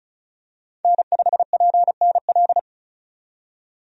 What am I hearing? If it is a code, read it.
Morse code, heard as a keyed tone, about 36 wpm, 700 Hz. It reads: N5PNL
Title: USA Call Signs - 5 Characters - 35wpm